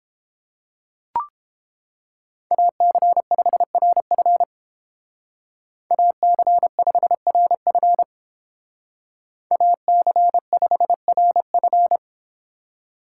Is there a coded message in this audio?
Yes: Morse code from 2.5 to 12.0 s, heard as a keyed tone, about 30 wpm, 700 Hz. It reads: AC5RF AC5RF AC5RF